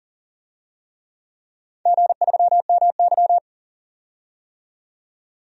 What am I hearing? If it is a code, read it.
Morse code, heard as a keyed tone, about 40 wpm, 700 Hz. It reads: G3MY
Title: DXCC-100 Call Signs - 4 Characters - 40wpm